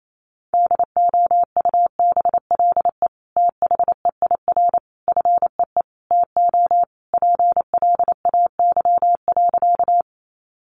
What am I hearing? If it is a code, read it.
Morse code, heard as a keyed tone, about 28 wpm, 700 Hz. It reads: DOUBLE THEIR FEE TO PLAY.
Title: Sentences from Top 1500 Words 28wpm